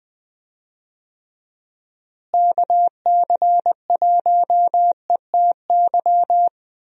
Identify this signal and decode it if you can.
Morse code, heard as a keyed tone, about 20 wpm, 700 Hz. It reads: KC1ETY
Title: USA Call Signs - 6 Characters - Repeated 3x - 20wpm